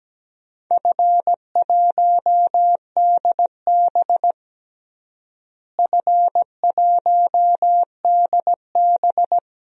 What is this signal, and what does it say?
Morse code, heard as a keyed tone, about 17 wpm, 700 Hz. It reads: F1DB F1DB
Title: DXCC-100 Call Signs - 4 Characters - Repeated 2x - 17wpm